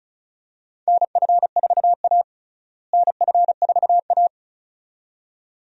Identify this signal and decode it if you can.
Morse code, heard as a keyed tone, about 34 wpm, 700 Hz. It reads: NF4A NF4A
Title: USA Call Signs - 4 Characters - Repeated 2x - 35wpm